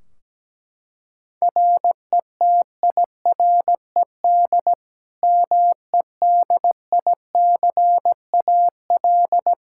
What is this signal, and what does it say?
Morse code, heard as a keyed tone, about 17 wpm, 700 Hz. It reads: RETIRED MEDICAL
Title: Practice QSO - Exchange 3 - More Info - 17wpm